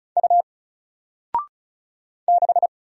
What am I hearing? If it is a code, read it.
Morse code, heard as a keyed tone, about 36 wpm, 700 Hz. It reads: U 6
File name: Single-Letter-Number-35wpm